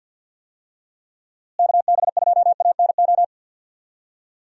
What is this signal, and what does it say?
Morse code, heard as a keyed tone, about 47 wpm, 700 Hz. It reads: KB2ANO
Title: USA Call Signs - 6 Characters - Repeated 3x - 50wpm